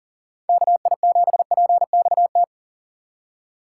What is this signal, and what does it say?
Morse code, heard as a keyed tone, about 37 wpm, 700 Hz. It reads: KI7PXT
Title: USA Call Signs - 6 Characters - 2x Character Spacing - 40wpm